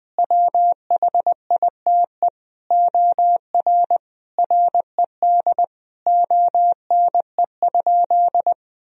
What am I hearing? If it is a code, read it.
Morse code, heard as a keyed tone, about 20 wpm, 700 Hz. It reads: WHITE OR RED ONE?
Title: Sentences from Top 1000 Words - Review Part 2 - 20wpm